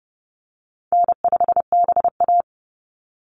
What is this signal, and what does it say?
Morse code, heard as a keyed tone, about 30 wpm, 700 Hz. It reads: N5BA
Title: Easy Call Signs Repeated 3x - 30wpm